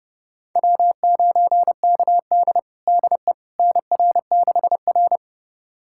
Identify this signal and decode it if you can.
Morse code, heard as a keyed tone, about 32 wpm, 700 Hz. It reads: W9KD DE NR6R
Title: USA Callsign DE Callsign - 4 Characters - 30wpm